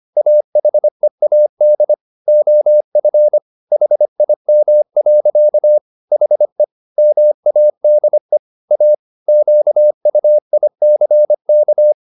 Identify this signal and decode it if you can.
Morse code, heard as a keyed tone, about 25 wpm, 600 Hz. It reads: AHEAD OF HIM. HE MADE A QUICK
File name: the old man and the sea part 1 25wpm